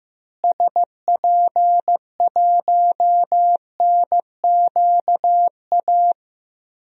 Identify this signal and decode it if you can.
Morse code, heard as a keyed tone, about 15 wpm, 700 Hz. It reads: SP1NQA
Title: DXCC-100 Call Signs - 6 Characters - Repeated 3x - 15wpm